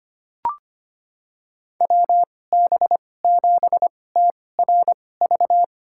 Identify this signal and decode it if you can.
Morse code, heard as a keyed tone, about 25 wpm, spaced as for 18 wpm, 700 Hz. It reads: WB7TRV
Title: USA Call Signs - 2x Character Spacing - 25wpm